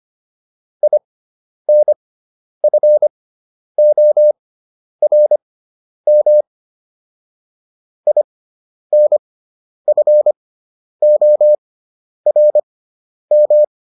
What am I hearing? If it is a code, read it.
Morse code, heard as a keyed tone, about 25 wpm, spaced as for 10 wpm, 600 Hz. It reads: INFORM INFORM